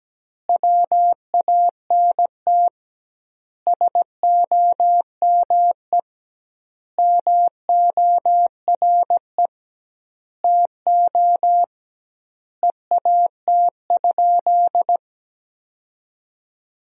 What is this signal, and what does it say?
Morse code, heard as a keyed tone, about 17 wpm, 700 Hz. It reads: WANT SOME MORE TO EAT?
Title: 2x Word-Spacing - Sentences from Top 300 Words - 17wpm